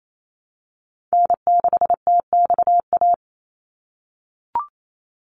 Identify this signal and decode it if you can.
Morse code, heard as a keyed tone, about 28 wpm, 700 Hz. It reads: N6TXA
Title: USA Call Signs 28wpm